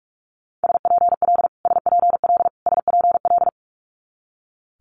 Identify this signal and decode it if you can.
Morse code, heard as a keyed tone, about 46 wpm, 700 Hz. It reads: SPL SPL SPL
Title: Multi-letter Phonemes Repeated 3x 45wpm